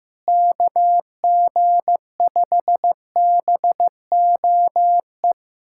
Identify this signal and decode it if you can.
Morse code, heard as a keyed tone, about 15 wpm, 700 Hz. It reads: KG5BOE
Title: USA Callsign DE Callsign - 6 Characters - 15wpm